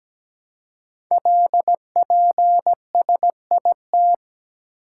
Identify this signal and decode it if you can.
Morse code, heard as a keyed tone, about 17 wpm, 700 Hz. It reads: LPSIT
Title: Code Groups - 5 Letters - 17wpm